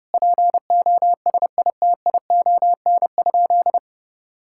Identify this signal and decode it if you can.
Morse code, heard as a keyed tone, about 30 wpm, 700 Hz. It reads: POSITION?
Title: Sentences from Top 1000 Words - Review Part 5 - 30wpm